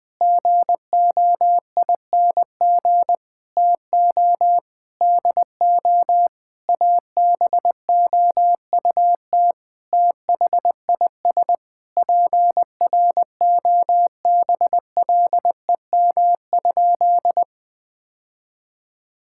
Morse code, heard as a keyed tone, about 21 wpm, 700 Hz. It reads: GOING TO DO ABOUT THIS PROBLEM?
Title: Sentences with 8-Word Phrases - 20wpm